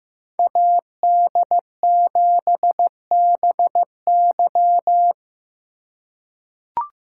Morse code, heard as a keyed tone, about 15 wpm, 700 Hz. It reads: AD7BY
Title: USA Call Signs - 5 Characters - 3x Character Spacing - 15wpm